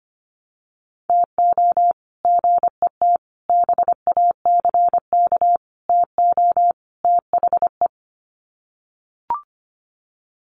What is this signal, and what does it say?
Morse code, heard as a keyed tone, about 25 wpm, 700 Hz. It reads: TO GET BACK TO THE
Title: Sets of 5 Words Encore 25wpm